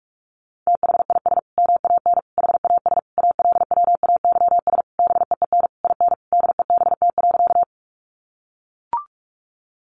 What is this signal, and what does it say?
Morse code, heard as a keyed tone, about 44 wpm, 700 Hz. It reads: THIS MAN HAS ALWAYS BEEN IN DEBT.
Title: Sentences from Top 1300 Words 45wpm